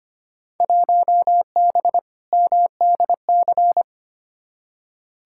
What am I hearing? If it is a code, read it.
Morse code, heard as a keyed tone, about 25 wpm, 700 Hz. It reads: 1B MDC
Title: Field Day Exchanges - 25wpm